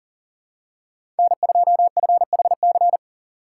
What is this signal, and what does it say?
Morse code, heard as a keyed tone, about 40 wpm, 700 Hz. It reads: N2FHC
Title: USA Call Signs - 5 Characters - 2x Character Spacing - 40wpm